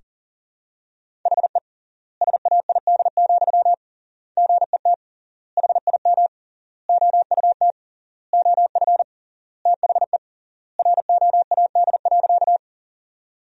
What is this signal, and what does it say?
Morse code, heard as a keyed tone, about 40 wpm, 700 Hz. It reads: HE SAID, GET HIM OUT OF THE ROAD.
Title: 3x Word-Spacing - Sentences from Top 400 Words - 40wpm